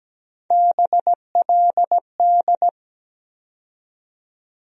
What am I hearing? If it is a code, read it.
Morse code, heard as a keyed tone, about 17 wpm, 700 Hz. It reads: BLD
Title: USA Call Sign Postfixes - 3 Letters 17wpm